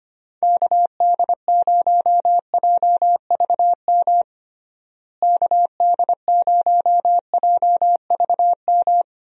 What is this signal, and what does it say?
Morse code, heard as a keyed tone, about 25 wpm, 700 Hz. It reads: KD0JVM KD0JVM
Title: USA Call Signs - 6 Characters - Repeated 2x - 25wpm